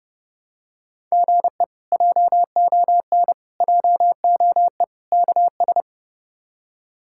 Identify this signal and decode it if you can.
Morse code, heard as a keyed tone, about 30 wpm, 700 Hz. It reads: GE JON JOE KS
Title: K1USN SST QSO Practice - 30wpm